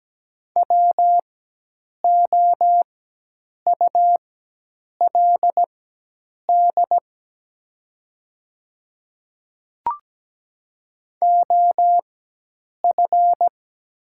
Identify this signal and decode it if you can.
Morse code, heard as a keyed tone, about 17 wpm, spaced as for 8 wpm, 690 Hz. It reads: WOULD OF
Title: 4x Character Spacing - Top 100 Words - 17wpm